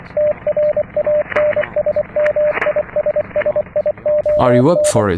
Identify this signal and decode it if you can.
Morse code, heard as a keyed tone, about 25 wpm, 600 Hz, with speech and radio noise. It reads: TRANSMISSION